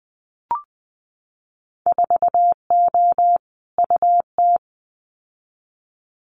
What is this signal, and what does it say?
Morse code, heard as a keyed tone, about 20 wpm, 700 Hz. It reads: 4O UT
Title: Winter Field Day Exchanges Encore 20wpm